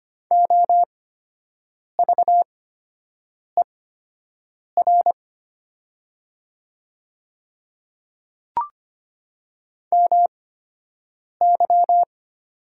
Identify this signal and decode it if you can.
Morse code, heard as a keyed tone, about 25 wpm, spaced as for 7 wpm, 700 Hz. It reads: OVER MY
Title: 8x Character Spacing - Top 100 Words - 25wpm